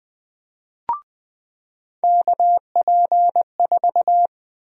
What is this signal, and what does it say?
Morse code, heard as a keyed tone, about 20 wpm, 700 Hz. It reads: KP4
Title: USA Call Sign Prefixes - 20wpm